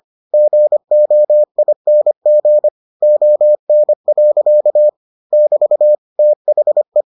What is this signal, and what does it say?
Morse code, heard as a keyed tone, about 25 wpm, 600 Hz. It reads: GOING ON. <BT> THE